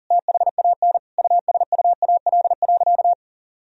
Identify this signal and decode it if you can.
Morse code, heard as a keyed tone, about 40 wpm, 700 Hz. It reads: THAN USUAL.
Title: Sentences from Top 1000 Words - Review Part 2 - 40wpm